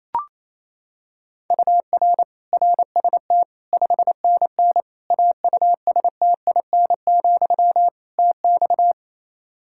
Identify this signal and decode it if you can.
Morse code, heard as a keyed tone, about 27 wpm, 700 Hz. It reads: UR RST 5NN AUSTIN, TX
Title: Practice QSO - Exchange 1 - Beginning - 28wpm